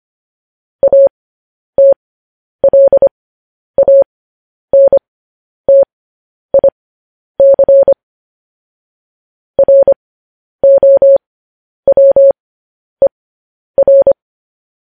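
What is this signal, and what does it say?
Morse code, heard as a keyed tone, about 25 wpm, spaced as for 10 wpm, 550 Hz. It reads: ATLANTIC ROWER
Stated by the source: Morse code news headlines broadcast.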